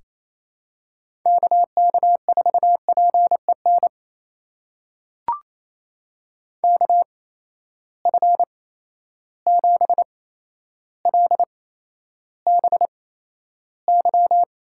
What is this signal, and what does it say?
Morse code, heard as a keyed tone, about 28 wpm, spaced as for 12 wpm, 700 Hz. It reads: KK4PEN KF7LBY
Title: USA Call Signs - 6 Characters - 8x Character Spacing - 28wpm